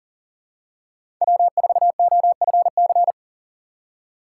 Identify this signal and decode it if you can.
Morse code, heard as a keyed tone, about 39 wpm, 700 Hz. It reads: W4OFC
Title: USA Call Signs - 5 Characters - 4x Character Spacing - 40wpm